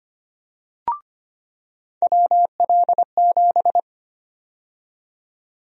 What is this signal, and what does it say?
Morse code, heard as a keyed tone, about 25 wpm, 700 Hz. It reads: WL7
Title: USA Call Sign Prefixes - 25wpm